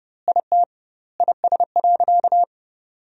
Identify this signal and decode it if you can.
Morse code, heard as a keyed tone, about 32 wpm, 700 Hz. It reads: IT IS.
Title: 2x Word-Spacing - Sentences from Top 100 Words - 30wpm